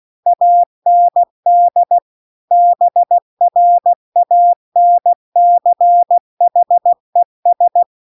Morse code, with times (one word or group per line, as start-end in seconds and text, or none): 0.3-2.0 AND
2.5-7.8 BRANCHES